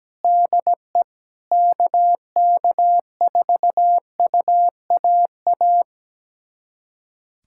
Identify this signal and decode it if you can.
Morse code, heard as a keyed tone, about 17 wpm, 700 Hz. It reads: DE KK4UAA
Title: USA Callsign DE Callsign - 6 Characters - 17wpm